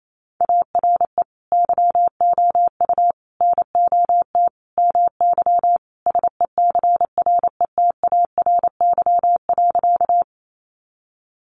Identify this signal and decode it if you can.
Morse code, heard as a keyed tone, about 28 wpm, 700 Hz. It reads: ARE YOU NOT MY SECRETARY.
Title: Sentences from Top 1300 Words 28wpm